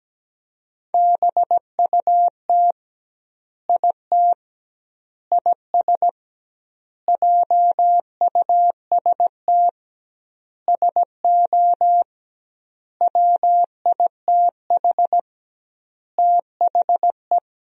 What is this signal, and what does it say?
Morse code, heard as a keyed tone, about 17 wpm, 700 Hz. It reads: BUT IT IS JUST SO WITH THE
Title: 2x Word-Spacing - Sentences from Top 200 Words - 17wpm